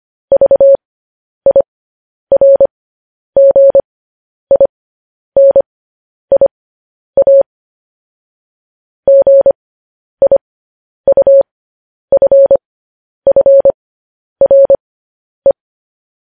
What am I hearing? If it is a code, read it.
Morse code, heard as a keyed tone, about 25 wpm, spaced as for 10 wpm, 550 Hz. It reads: VIRGINIA GIUFFRE
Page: news headlines in morse code at 10 wpm